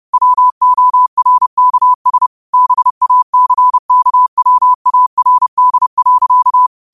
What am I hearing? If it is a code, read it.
Morse code, heard as a keyed tone, about 30 wpm, 1000 Hz. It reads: WORKS BACKWARD.